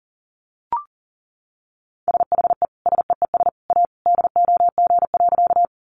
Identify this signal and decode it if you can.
Morse code, heard as a keyed tone, about 37 wpm, 700 Hz. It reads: SHE SEES A DOG.
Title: Easy Phrases 40wpm